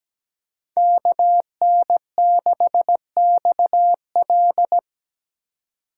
Morse code, heard as a keyed tone, about 17 wpm, 700 Hz. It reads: KN6XL
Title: USA Call Signs - 5 Characters - 4x Character Spacing - 17wpm